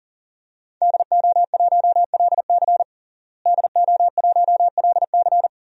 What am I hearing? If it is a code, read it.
Morse code, heard as a keyed tone, about 40 wpm, 700 Hz. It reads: DO1LC DO1LC
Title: DXCC-100 Call Signs - 5 Characters - Repeated 2x - 40wpm